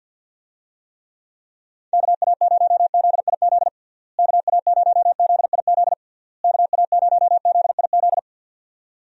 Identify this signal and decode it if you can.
Morse code, heard as a keyed tone, about 50 wpm, 700 Hz. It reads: KA0ZIZ KA0ZIZ KA0ZIZ
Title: USA Call Signs - 6 Characters - Repeated 3x - 50wpm